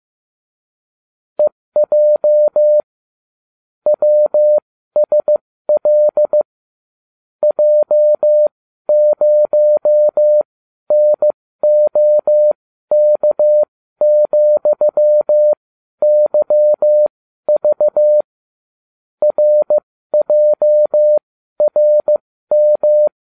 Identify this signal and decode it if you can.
Morse code, heard as a keyed tone, about 15 wpm, spaced as for 12 wpm, 600 Hz. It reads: EJ WSL J0NOK,YV RJRM